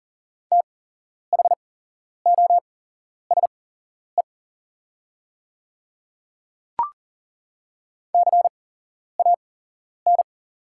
Morse code, heard as a keyed tone, about 37 wpm, spaced as for 11 wpm, 700 Hz. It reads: THOSE CAN
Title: 8x Character Spacing - Top 300 Words - 40wpm